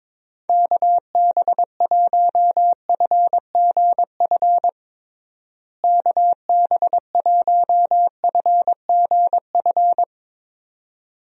Morse code, heard as a keyed tone, about 22 wpm, 700 Hz. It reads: KB1FGF KB1FGF
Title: USA Call Signs - 6 Characters - Repeated 2x - 22wpm